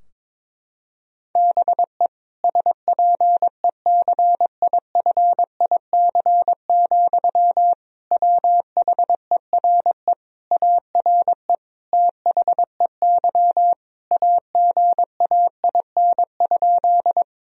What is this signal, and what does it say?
Morse code, heard as a keyed tone, about 22 wpm, 700 Hz. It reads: BE SPECIFIC, WHERE ARE THEY AGAIN?
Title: Sentences from Top 1000 Words - Review Part 3 - 22wpm